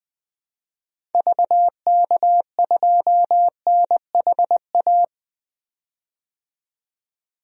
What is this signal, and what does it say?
Morse code, heard as a keyed tone, about 20 wpm, 700 Hz. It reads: VK2NHA
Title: DXCC-100 Call Signs - 6 Characters - 20wpm